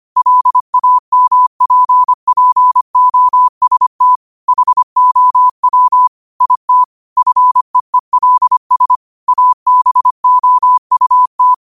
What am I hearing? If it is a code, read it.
Morse code, heard as a keyed tone, about 25 wpm, 1000 Hz. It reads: LAMPPOST HOW IT FEELS ABOUT
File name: qotdfiles / 25wpm